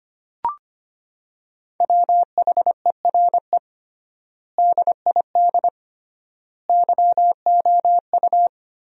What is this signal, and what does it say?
Morse code, heard as a keyed tone, about 25 wpm, 700 Hz. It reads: WHERE DID YOU
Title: 3x Word-Spacing - Sentences from Top 200 Words - 25wpm